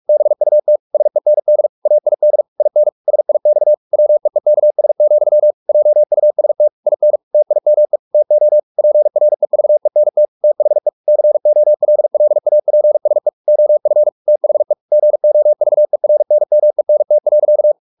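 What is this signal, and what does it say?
Morse code, heard as a keyed tone, about 44 wpm, 600 Hz. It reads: BUT SEND AID IN SIX WEEKS, JUST IN TIME TO PREVENT THE COLLAPSE OF THE GOVERNMENT.
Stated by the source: Morse practice at 45 wpm, the novel Never by Ken Follett in English, from CW med Gnister.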